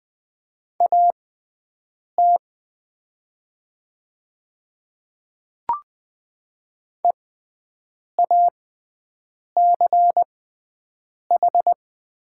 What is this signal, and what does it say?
Morse code, heard as a keyed tone, about 20 wpm, spaced as for 7 wpm, 690 Hz. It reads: AT EACH
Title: 6x Character Spacing - Top 200 Words - 20wpm